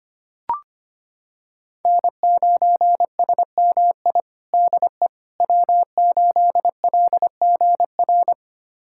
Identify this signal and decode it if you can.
Morse code, heard as a keyed tone, about 25 wpm, 700 Hz. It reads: N9SMI DE W8LGR
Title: USA Callsign DE Callsign - 5 Characters - 25wpm